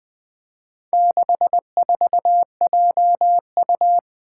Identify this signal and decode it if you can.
Morse code, heard as a keyed tone, about 21 wpm, 700 Hz. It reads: 64JU
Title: Four Challenging Letter-Number Combinations 20wpm